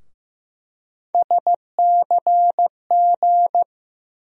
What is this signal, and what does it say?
Morse code, heard as a keyed tone, about 15 wpm, 700 Hz. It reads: SCG
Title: USA Call Sign Postfixes - 3 Letters 15wpm